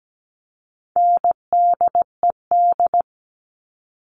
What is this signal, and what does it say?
Morse code, heard as a keyed tone, about 17 wpm, 700 Hz. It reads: NDED
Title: All Cut Numbers - 4-Digit 17wpm